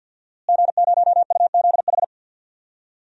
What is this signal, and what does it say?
Morse code, heard as a keyed tone, about 53 wpm, 700 Hz. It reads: G0UZH
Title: DXCC-100 Call Signs - 5 Characters - Repeated 2x - 50wpm